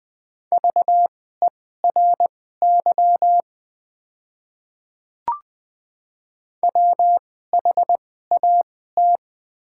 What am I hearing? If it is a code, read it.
Morse code, heard as a keyed tone, about 20 wpm, spaced as for 14 wpm, 700 Hz. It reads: VERY WHAT
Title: 2x Character Spacing - Top 200 Words - 20wpm